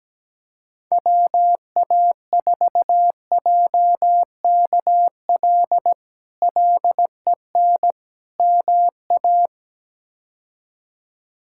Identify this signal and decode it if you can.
Morse code, heard as a keyed tone, about 17 wpm, 700 Hz. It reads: WA4JKL LEN MA
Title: K1USN SST QSO Practice - 17wpm